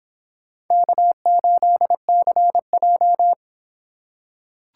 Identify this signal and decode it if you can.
Morse code, heard as a keyed tone, about 24 wpm, 700 Hz. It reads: K8CJ